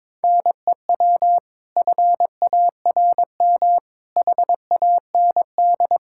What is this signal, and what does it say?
Morse code, heard as a keyed tone, about 22 wpm, 700 Hz. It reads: NEW FARM HAND
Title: Pangram Challenge v2 - 22wpm